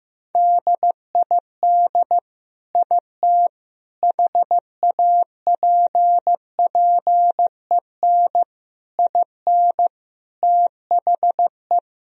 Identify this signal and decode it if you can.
Morse code, heard as a keyed tone, about 15 wpm, 700 Hz. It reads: DID IT HAPPEN IN THE
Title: Sentences from Top 1000 Words - Review Part 2 - 15wpm